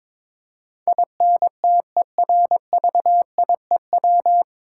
Morse code, heard as a keyed tone, about 22 wpm, 700 Hz. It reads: INTERVIEW
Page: Morse Code Ninja practice 22wpm